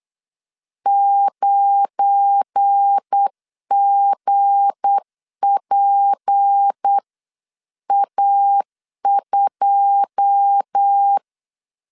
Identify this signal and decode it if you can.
Morse code, heard as a keyed tone, about 8 wpm, 800 Hz. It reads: 9GPA2
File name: 8wpm